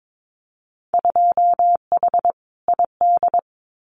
Morse code, heard as a keyed tone, about 22 wpm, 700 Hz. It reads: 2H ID
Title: Winter Field Day Exchanges 22wpm